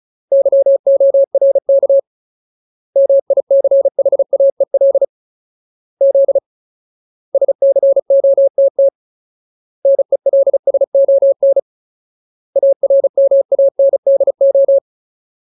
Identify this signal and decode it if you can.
Morse code, heard as a keyed tone, about 34 wpm, 550 Hz. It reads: YORK MICHAEL Z SCOTT NELSON ARMANDO
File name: Ultimate Word List / 35WPM Ultimate